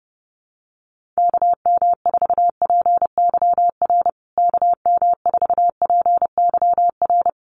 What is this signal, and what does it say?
Morse code, heard as a keyed tone, about 30 wpm, 700 Hz. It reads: KM4PYR KM4PYR
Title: USA Call Signs Repeated 2x 30wpm